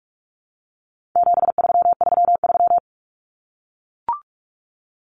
Morse code, heard as a keyed tone, about 44 wpm, 700 Hz. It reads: 7333
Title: Numbers - 4-Digit 45wpm